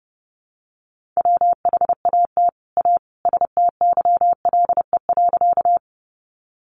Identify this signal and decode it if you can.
Morse code, heard as a keyed tone, about 30 wpm, 700 Hz. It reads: WHAT A STYLE.
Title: Sentences from Top 800 Words 30wpm